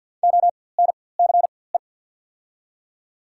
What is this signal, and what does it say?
Morse code, heard as a keyed tone, about 50 wpm, spaced as for 23 wpm, 690 Hz. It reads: ONCE
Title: 4x Character Spacing - Top 500 Words - 50wpm